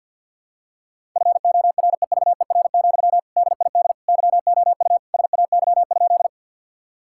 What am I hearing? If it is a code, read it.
Morse code, heard as a keyed tone, about 50 wpm, 700 Hz. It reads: FOREVER, DID YOU SAY?
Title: Sentences from Top 1600 Words 50wpm